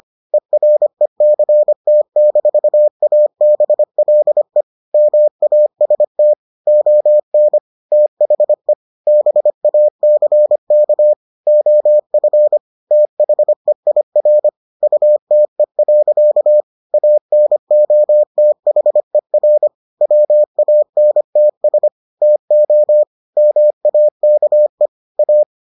Morse code, heard as a keyed tone, about 25 wpm, 600 Hz. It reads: ERECT-ABLE MAST ON THE BACK OF THEIR UTE. ANOTHER WANTS TO MAKE A